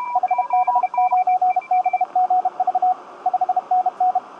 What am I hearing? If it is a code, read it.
Morse code, heard as a keyed tone, about 32 wpm, 750 Hz, with radio noise. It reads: HB9BGV 5NN
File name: tt8tt-10m-cw-here-in-jn90ab